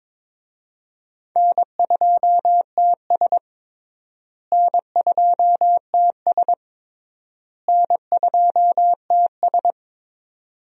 Morse code, heard as a keyed tone, about 22 wpm, 700 Hz. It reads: N2TS N2TS N2TS
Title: USA Call Signs - 4 Characters - Repeated 3x - 22wpm